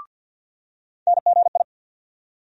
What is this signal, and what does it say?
Morse code, heard as a keyed tone, about 49 wpm, 700 Hz. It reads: NGI
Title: USA Call Sign Postfixes - 3 Letters 50wpm